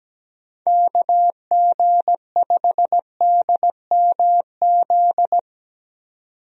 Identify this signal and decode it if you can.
Morse code, heard as a keyed tone, about 17 wpm, 700 Hz. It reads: KG5DMZ
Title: USA Call Signs - 6 Characters - 17wpm